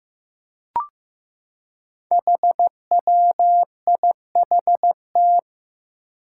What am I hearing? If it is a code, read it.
Morse code, heard as a keyed tone, about 15 wpm, 700 Hz. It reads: HWIHT
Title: Five Challenging Letter-Number Combinations 15wpm